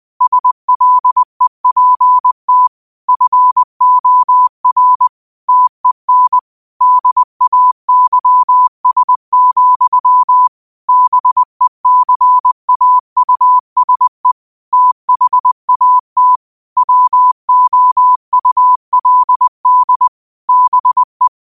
Morse code, heard as a keyed tone, about 20 wpm, 1000 Hz. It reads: SLEPT FOR TEN DAYS, BECAUSE THAT WOULD BE
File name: qotdfiles / 20wpm